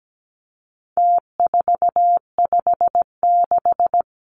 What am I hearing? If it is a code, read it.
Morse code, heard as a keyed tone, about 17 wpm, 700 Hz. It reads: T456
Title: Numbers - Contest Style - 4-Digit 17wpm